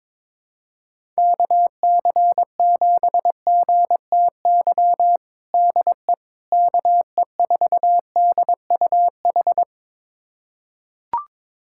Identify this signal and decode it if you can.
Morse code, heard as a keyed tone, about 22 wpm, 700 Hz. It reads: KC7GTY DE KE4DUH
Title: USA Callsign DE Callsign - 6 Characters - 22wpm